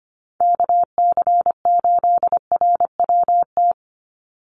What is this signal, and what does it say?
Morse code, heard as a keyed tone, about 25 wpm, 700 Hz. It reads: KC8RWT
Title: USA Call Signs 25wpm